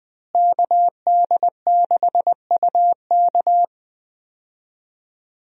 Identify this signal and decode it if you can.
Morse code, heard as a keyed tone, about 20 wpm, 700 Hz. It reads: KD6UK
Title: USA Call Signs - 5 Characters - Repeated 2x - 20wpm